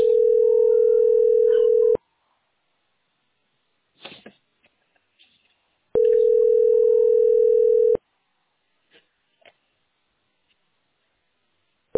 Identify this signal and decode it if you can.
Ringback tone (North America), 440 + 480 Hz.